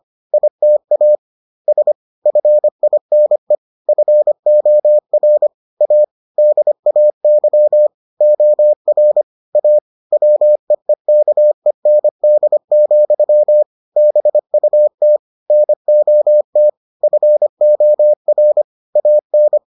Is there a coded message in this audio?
Yes: Morse code from 0.3 to 19.6 s, heard as a keyed tone, about 25 wpm, 600 Hz. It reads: ITAS FINE FOR A DAY OR A WEEKEND, BUT NOT FOR AN